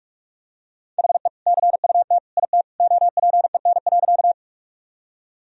Morse code, heard as a keyed tone, about 46 wpm, 700 Hz. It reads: HE CUT IT OPEN.
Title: Sentences from Top 1000 Words - Review Part 1 - 45wpm